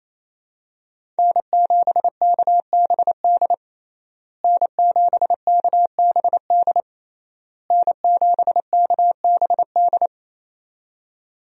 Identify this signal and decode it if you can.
Morse code, heard as a keyed tone, about 27 wpm, 700 Hz. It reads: N7KBD N7KBD N7KBD
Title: USA Call Signs - 5 Characters - Repeated 3x - 28wpm